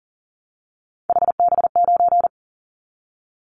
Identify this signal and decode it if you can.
Morse code, heard as a keyed tone, about 38 wpm, 700 Hz. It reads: HB9
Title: Callsign Prefixes for Top 25 DXCC - Repeated 3x - 40wpm